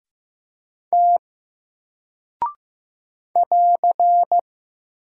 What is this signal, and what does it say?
Morse code, heard as a keyed tone, about 15 wpm, 700 Hz. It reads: T <AR>
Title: Single Character (Letter:Number:Punctuation:Prosign) Alt 15wpm